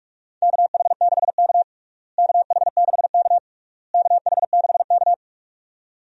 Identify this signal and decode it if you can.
Morse code, heard as a keyed tone, about 42 wpm, 700 Hz. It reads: KH6K KH6K KH6K
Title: DXCC-100 Call Signs - 4 Characters - Repeated 3x - 45wpm